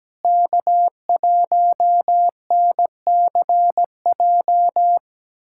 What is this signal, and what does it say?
Morse code, heard as a keyed tone, about 17 wpm, 700 Hz. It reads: K1NCJ
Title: USA Callsign DE Callsign - 5 Characters - 17wpm